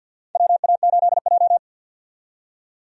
Morse code, heard as a keyed tone, about 53 wpm, 700 Hz. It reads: WA8J
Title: USA Call Signs - 4 Characters - 50wpm